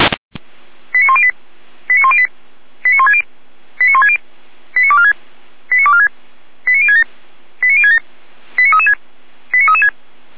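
Five-tone selective call (ZVEI), heard as quick runs of five tones, tones of about 70 ms. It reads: 89108 89108 89160 89160 89246 89246 89067 89067 89336 89336
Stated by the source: ZVEI five-tone call received by SDR.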